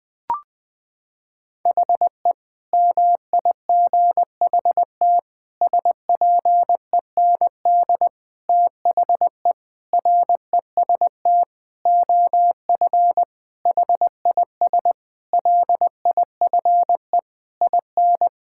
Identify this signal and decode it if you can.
Morse code, heard as a keyed tone, about 20 wpm, 700 Hz. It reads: HE MIGHT SPEND THE REST OF HIS LIFE IN
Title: Sentences with 8-Word Phrases - 20wpm